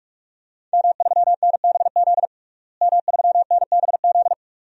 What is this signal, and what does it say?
Morse code, heard as a keyed tone, about 44 wpm, 700 Hz. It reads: M3NBZ M3NBZ
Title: DXCC-100 Call Signs - 5 Characters - Repeated 2x - 45wpm